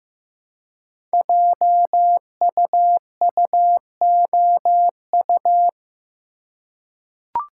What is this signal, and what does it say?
Morse code, heard as a keyed tone, about 15 wpm, 700 Hz. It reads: JUUOU
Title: Code Groups - 5 Letters - 15wpm